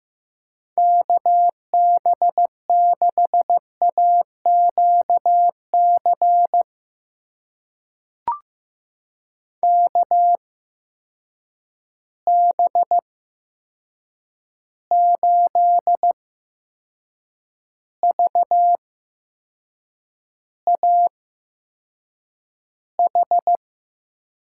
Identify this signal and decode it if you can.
Morse code, heard as a keyed tone, about 15 wpm, spaced as for 6 wpm, 700 Hz. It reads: KB6AQC KB8VAH
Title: USA Call Signs - 6 Characters - 8x Character Spacing - 15wpm